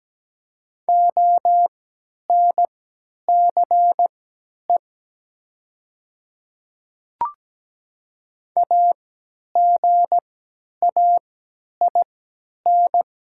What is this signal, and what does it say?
Morse code, heard as a keyed tone, about 17 wpm, spaced as for 10 wpm, 700 Hz. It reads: ONCE AGAIN
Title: 3x Character Spacing - Top 300 Words - 17wpm